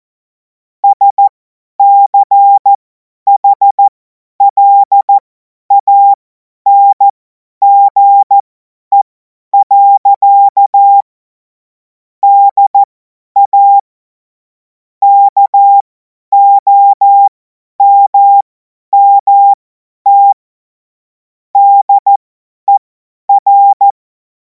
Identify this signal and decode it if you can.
Morse code, heard as a keyed tone, about 14 wpm, spaced as for 10 wpm, 800 Hz. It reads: SCHLANGE. DA KOMMT DER